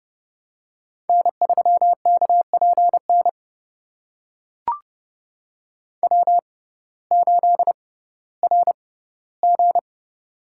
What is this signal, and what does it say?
Morse code, heard as a keyed tone, about 30 wpm, spaced as for 17 wpm, 700 Hz. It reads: N3KPN W8RG
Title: USA Call Signs - 6x Character Spacing - 30wpm